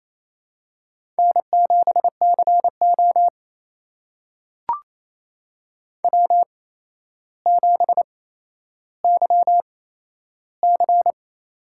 Morse code, heard as a keyed tone, about 28 wpm, spaced as for 12 wpm, 700 Hz. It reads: N7CO W7YC
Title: USA Call Signs - 4 Characters - 8x Character Spacing - 28wpm